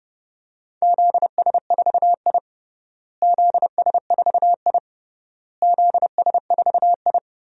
Morse code, heard as a keyed tone, about 32 wpm, 700 Hz. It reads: ZS4I ZS4I ZS4I
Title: DXCC-100 Call Signs - 4 Characters - Repeated 3x - 30wpm